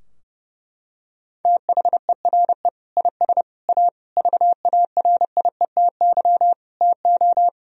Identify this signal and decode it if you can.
Morse code, heard as a keyed tone, about 30 wpm, 700 Hz. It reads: THERE IS A VARIETY TO
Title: Sentences from Top 1000 Words - Review Part 4 - 30wpm